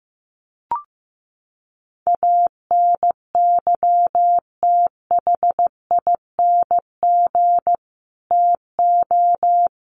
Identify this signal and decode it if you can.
Morse code, heard as a keyed tone, about 15 wpm, 700 Hz. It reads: ANYTHING TO
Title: Sets of 4 Words 15wpm